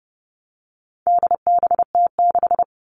Morse code, heard as a keyed tone, about 30 wpm, 700 Hz. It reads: DBT6
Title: All Cut Numbers - 4-Digit 30wpm